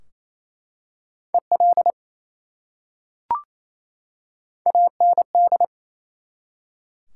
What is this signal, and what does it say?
Morse code, heard as a keyed tone, about 28 wpm, 700 Hz. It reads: EL AND
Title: Top 180 QSO Elements 28wpm